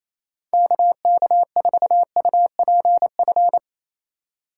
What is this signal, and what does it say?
Morse code, heard as a keyed tone, about 28 wpm, 700 Hz. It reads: KK4UPF
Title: USA Call Signs - 6 Characters - 4x Character Spacing - 28wpm